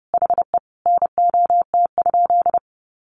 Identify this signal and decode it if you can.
Morse code, heard as a keyed tone, about 30 wpm, 700 Hz. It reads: HE NOT?
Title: Sentences from Top 1500 Words 30wpm